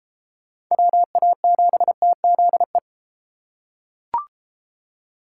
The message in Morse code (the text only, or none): WA7TZE